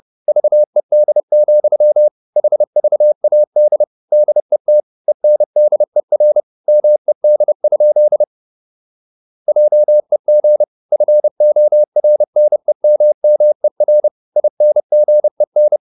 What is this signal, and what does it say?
Morse code, heard as a keyed tone, about 30 wpm, 600 Hz. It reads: VED, HVAD DET ENDER MED? JEG FORNEMMER INGEN
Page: Ferie 30 wpm